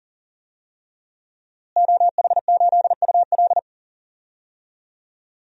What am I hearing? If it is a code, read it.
Morse code, heard as a keyed tone, about 40 wpm, 700 Hz. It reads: OH8UL
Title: DXCC-100 Call Signs - 5 Characters - Repeated 2x - 40wpm